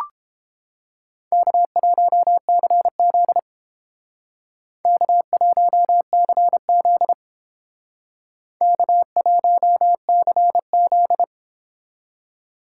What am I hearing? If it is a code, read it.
Morse code, heard as a keyed tone, about 29 wpm, 700 Hz. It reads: K1CZ K1CZ K1CZ